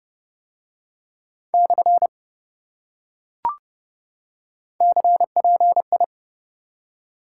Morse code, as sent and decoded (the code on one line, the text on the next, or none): -..-. / -.-. .--. ..
/ CPI